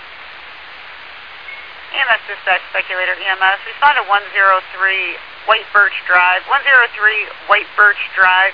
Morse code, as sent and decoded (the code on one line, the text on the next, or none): none